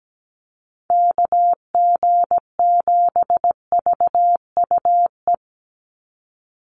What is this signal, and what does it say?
Morse code, heard as a keyed tone, about 17 wpm, 700 Hz. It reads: KG7VUE
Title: USA Call Signs 17wpm